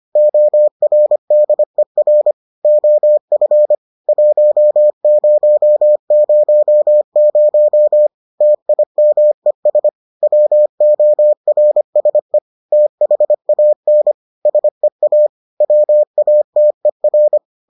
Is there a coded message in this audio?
Yes: Morse code from 0.2 to 17.4 s, heard as a keyed tone, about 25 wpm, 600 Hz. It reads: ORDER OF 1000 TIMES WORSE THAN SEA WATER